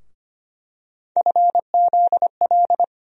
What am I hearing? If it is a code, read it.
Morse code, heard as a keyed tone, about 25 wpm, 700 Hz. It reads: FZL
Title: USA Call Sign Postfixes - 3 Letters 25wpm